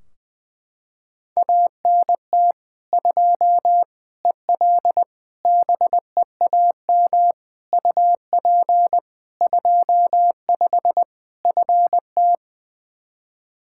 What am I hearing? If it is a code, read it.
Morse code, heard as a keyed tone, about 20 wpm, 700 Hz. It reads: ANT 2 EL BEAM UP 25 FT
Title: Practice QSO - Exchange 2 - Info - 20wpm